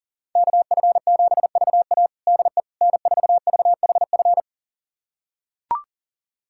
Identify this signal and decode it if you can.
Morse code, heard as a keyed tone, about 40 wpm, 700 Hz. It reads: KF7VA DE N4VHF
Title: USA Callsign DE Callsign - 5 Characters - 40wpm